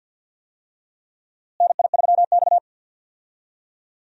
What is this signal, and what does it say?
Morse code, heard as a keyed tone, about 49 wpm, 700 Hz. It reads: NI3X
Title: USA Call Signs - 4 Characters - Repeated 2x - 50wpm